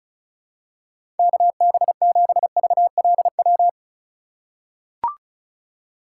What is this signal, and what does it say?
Morse code, heard as a keyed tone, about 34 wpm, 700 Hz. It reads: KB7VLW
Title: USA Call Signs - 6 Characters - 2x Character Spacing - 35wpm